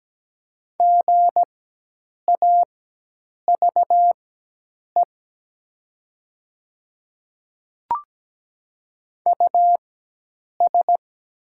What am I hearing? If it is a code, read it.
Morse code, heard as a keyed tone, about 17 wpm, spaced as for 8 wpm, 700 Hz. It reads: GAVE US
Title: 4x Character Spacing - Top 500 Words - 17wpm